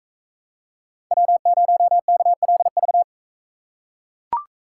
Morse code, heard as a keyed tone, about 44 wpm, 700 Hz. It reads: W0KLV